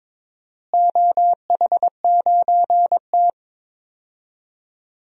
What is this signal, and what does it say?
Morse code, heard as a keyed tone, about 21 wpm, 700 Hz. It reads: OH9T
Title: DXCC-100 Call Signs - 4 Characters - Repeated 2x - 22wpm